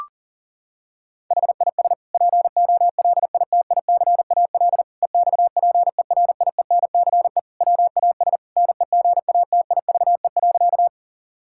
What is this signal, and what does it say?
Morse code, heard as a keyed tone, about 40 wpm, 700 Hz. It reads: HIS POLITICAL EXPERIENCE WAS NEGATIVE.
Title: Sentences from Top 1000 Words - Review Part 6 - 40wpm